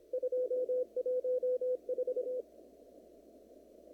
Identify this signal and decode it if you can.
Morse code, heard as a keyed tone, about 25 wpm, 500 Hz. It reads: 214
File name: CW / kt5j wpx cw-op-k5pi